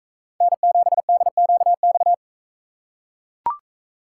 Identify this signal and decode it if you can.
Morse code, heard as a keyed tone, about 38 wpm, 700 Hz. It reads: N7DQX